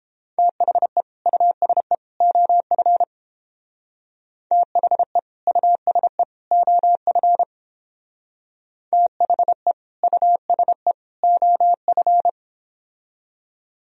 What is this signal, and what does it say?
Morse code, heard as a keyed tone, about 31 wpm, 700 Hz. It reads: THE USE OF THE USE OF THE USE OF